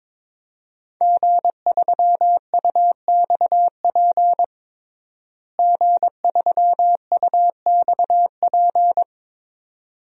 Morse code, heard as a keyed tone, about 22 wpm, 700 Hz. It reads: G3UXP G3UXP
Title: DXCC-100 Call Signs - 5 Characters - Repeated 2x - 22wpm